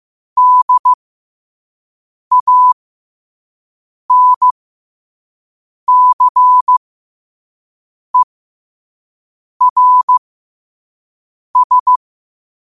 Morse code, heard as a keyed tone, about 15 wpm, spaced as for 5 wpm, 1000 Hz. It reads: DANCERS
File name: qotdfiles / 5wpm